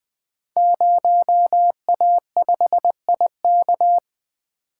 Morse code, heard as a keyed tone, about 20 wpm, 700 Hz. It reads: 0A5IK
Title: Code Groups - 5 Letter:Numbers - 20wpm